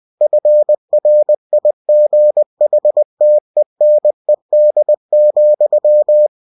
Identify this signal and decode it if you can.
Morse code, heard as a keyed tone, about 20 wpm, 600 Hz. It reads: FRIGHTENED,